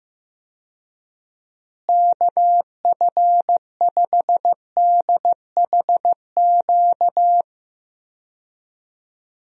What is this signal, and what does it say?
Morse code, heard as a keyed tone, about 15 wpm, 700 Hz. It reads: KF5DHQ
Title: USA Call Signs - 6 Characters - Repeated 3x - 15wpm